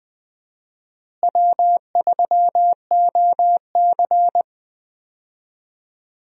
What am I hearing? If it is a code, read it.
Morse code, heard as a keyed tone, about 20 wpm, 700 Hz. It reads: W3OC
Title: USA Call Signs - 4 Characters - Repeated 2x - 20wpm